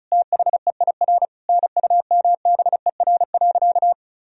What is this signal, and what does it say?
Morse code, heard as a keyed tone, about 34 wpm, 700 Hz. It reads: THEIR NUMBER.
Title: Sentences from Top 1000 Words - Review Part 1 - 35wpm